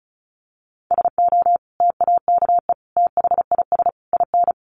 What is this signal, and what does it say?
Morse code, heard as a keyed tone, about 34 wpm, 700 Hz. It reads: SO TAKE THIS IN
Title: Sentences from Top 1500 Words 35wpm